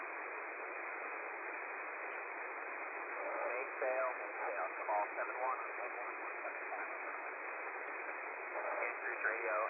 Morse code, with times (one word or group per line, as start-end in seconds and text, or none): none